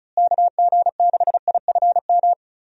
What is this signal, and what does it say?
Morse code, heard as a keyed tone, about 36 wpm, 700 Hz. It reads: KG6IFM
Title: USA Call Signs - 6 Characters - Repeated 2x - 35wpm